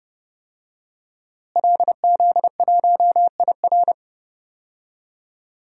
Morse code, heard as a keyed tone, about 31 wpm, 700 Hz. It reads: LZ1IR
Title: DXCC-100 Call Signs - 5 Characters - Repeated 3x - 30wpm